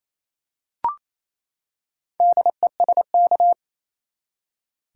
Morse code, heard as a keyed tone, about 28 wpm, 700 Hz. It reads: DESK
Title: Four Letter Words - 28wpm